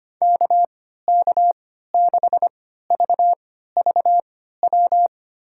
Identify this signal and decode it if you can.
Morse code, heard as a keyed tone, about 25 wpm, spaced as for 14 wpm, 700 Hz. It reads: KK6VVW
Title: USA Call Signs - 3x Character Spacing - 25wpm